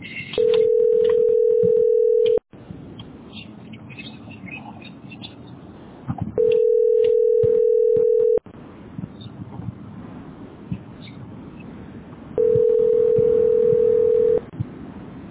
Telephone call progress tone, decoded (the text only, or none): ringback tone